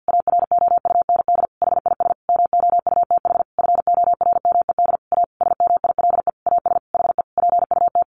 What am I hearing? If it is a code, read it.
Morse code, heard as a keyed tone, about 50 wpm, 700 Hz. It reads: AROUND HIS MOUTH FORMED A SMILE AS HE PUT
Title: A Sound Of Thunder 50wpm